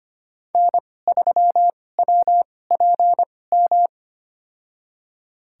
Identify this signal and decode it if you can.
Morse code, heard as a keyed tone, about 25 wpm, spaced as for 18 wpm, 700 Hz. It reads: N3WPM
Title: USA Call Signs - 5 Characters - 2x Character Spacing - 25wpm